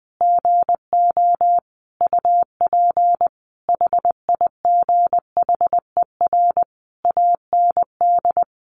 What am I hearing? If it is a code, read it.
Morse code, heard as a keyed tone, about 20 wpm, 700 Hz. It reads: GO UP HIGHER AND
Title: Additional Easy Phrases 20wpm